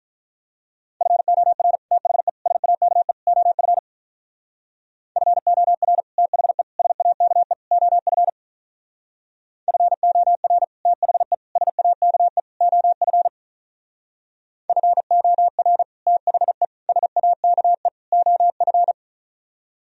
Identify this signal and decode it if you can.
Morse code, heard as a keyed tone, about 42 wpm, 700 Hz. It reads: FOR THE SAKE OF FOR THE SAKE OF FOR THE SAKE OF FOR THE SAKE OF
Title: Speed-Racing - Sets of 4 Words 35wpm